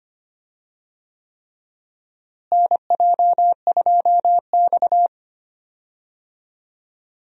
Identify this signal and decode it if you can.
Morse code, heard as a keyed tone, about 25 wpm, 700 Hz. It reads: NJ2X
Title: USA Call Signs - 4 Characters - Repeated 3x - 25wpm